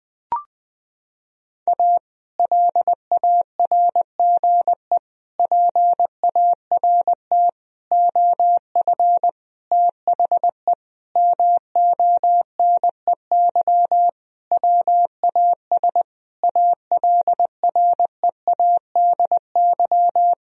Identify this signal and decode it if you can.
Morse code, heard as a keyed tone, about 20 wpm, 700 Hz. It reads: A LARGE PART OF THE MONEY WAS ALREADY
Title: Sentences from Top 1000 Words - Review Part 1 - 20wpm